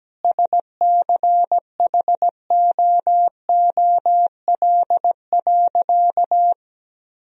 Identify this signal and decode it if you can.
Morse code, heard as a keyed tone, about 17 wpm, 700 Hz. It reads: SCHOOL.
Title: Sentences from Top 1000 Words - Review Part 1 - 17wpm